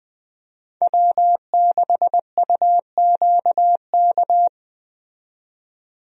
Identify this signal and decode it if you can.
Morse code, heard as a keyed tone, about 20 wpm, 700 Hz. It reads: W6UQK
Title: USA Call Signs - 5 Characters - 20wpm